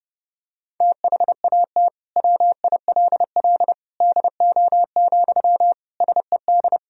Morse code, heard as a keyed tone, about 30 wpm, 700 Hz. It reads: THAT WILL DO, SED
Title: Sentences from Top 100 Words and Abbreviations 30wpm